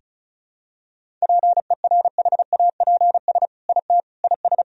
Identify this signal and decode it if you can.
Morse code, heard as a keyed tone, about 36 wpm, 700 Hz. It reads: PERHAPS IT IS
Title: Sentences from Top 1000 Words - Review Part 2 - 35wpm